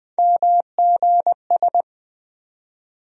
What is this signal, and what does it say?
Morse code, heard as a keyed tone, about 20 wpm, 700 Hz. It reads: MGS
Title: USA Call Sign Postfixes - 3 Letters 20wpm